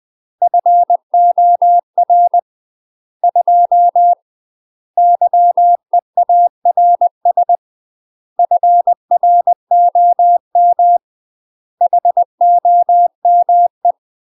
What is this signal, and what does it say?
Morse code, heard as a keyed tone, about 20 wpm, 700 Hz. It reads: FOR 2 YEARS FROM HOME
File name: UK MASTER QSOs TRAINING / 20WPM / UK MASTER QSOs 0012